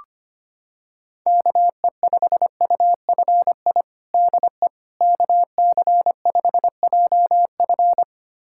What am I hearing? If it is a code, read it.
Morse code, heard as a keyed tone, about 25 wpm, 700 Hz. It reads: KE5UFI DE KC5JF
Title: USA Callsign DE Callsign 25wpm